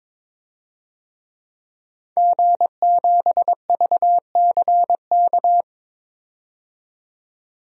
Morse code, heard as a keyed tone, about 22 wpm, 700 Hz. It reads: G7VCK
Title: DXCC-100 Call Signs - 5 Characters - 22wpm